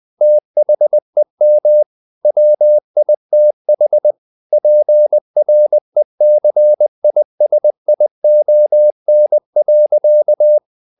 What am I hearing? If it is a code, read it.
Morse code, heard as a keyed tone, about 20 wpm, 600 Hz. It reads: THEM WITH PRECISION.
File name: the old man and the sea / the old man and the sea part 1 20wpm